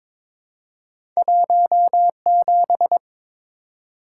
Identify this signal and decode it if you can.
Morse code, heard as a keyed tone, about 23 wpm, 700 Hz. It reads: 17
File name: practice / 22wpm / Two Letter-Number Combinations - 22wpm